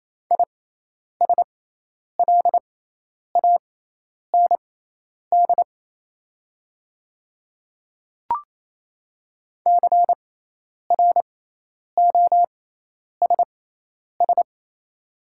Morse code, heard as a keyed tone, about 28 wpm, spaced as for 10 wpm, 690 Hz. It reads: ISLAND CROSS
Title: 6x Character Spacing - Top 500 Words - 28wpm